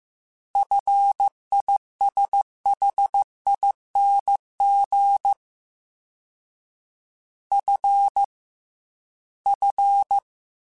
Morse code, heard as a keyed tone, about 15 wpm, 800 Hz. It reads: FISHING F F
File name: Fs / Single F mp3s / Single F Prefixes 3x2 Twice @13